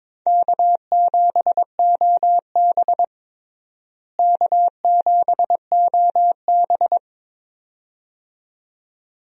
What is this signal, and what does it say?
Morse code, heard as a keyed tone, about 22 wpm, 700 Hz. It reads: K7OB K7OB
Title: USA Call Signs - 4 Characters - Repeated 2x - 22wpm